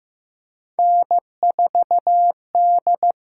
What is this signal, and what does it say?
Morse code, heard as a keyed tone, about 16 wpm, 700 Hz. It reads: N4D
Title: USA Call Signs - 5 Characters - 6x Character Spacing - 15wpm